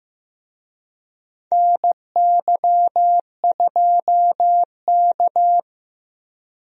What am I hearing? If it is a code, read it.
Morse code, heard as a keyed tone, about 15 wpm, 700 Hz. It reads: NY2K
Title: USA Call Signs - 4 Characters - Repeated 3x - 15wpm